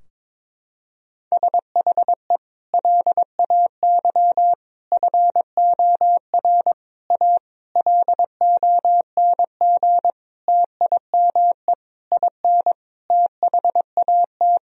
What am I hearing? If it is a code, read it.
Morse code, heard as a keyed tone, about 22 wpm, 700 Hz. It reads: SHE LAY FOR A LONG TIME IN THAT
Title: Sentences from Top 1000 Words - Review Part 2 - 22wpm